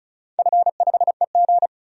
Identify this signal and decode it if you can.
Morse code, heard as a keyed tone, about 36 wpm, 700 Hz. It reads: F5EG
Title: DXCC-100 Call Signs - 4 Characters - 35wpm